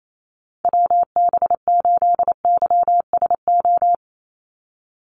Morse code, heard as a keyed tone, about 28 wpm, 700 Hz. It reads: WB8YSO
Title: USA Call Signs 28wpm